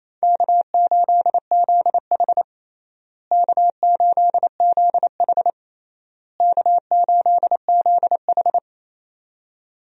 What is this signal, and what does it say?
Morse code, heard as a keyed tone, about 28 wpm, 700 Hz. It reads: K8ZH K8ZH K8ZH
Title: USA Call Signs - 4 Characters - Repeated 3x - 28wpm